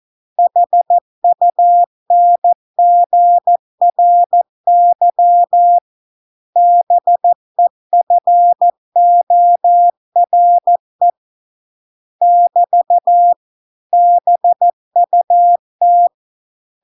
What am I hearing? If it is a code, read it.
Morse code, heard as a keyed tone, about 14 wpm, 700 Hz. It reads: HUNGRY BEFORE <BT> BUT